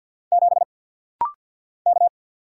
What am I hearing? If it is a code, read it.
Morse code, heard as a keyed tone, about 50 wpm, 700 Hz. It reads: 7 K
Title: Single Letter-Number Alt - Rapid-Fire 50wpm